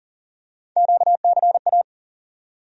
Morse code, heard as a keyed tone, about 40 wpm, 700 Hz. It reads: QCA
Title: USA Call Sign Postfixes - 3 Letters 40wpm